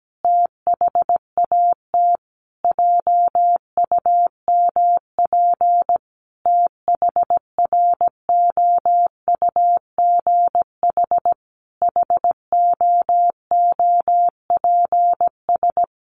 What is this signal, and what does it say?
Morse code, heard as a keyed tone, about 17 wpm, 700 Hz. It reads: THAT JUMP THROUGH HOOPS
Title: Easy Phrases 17wpm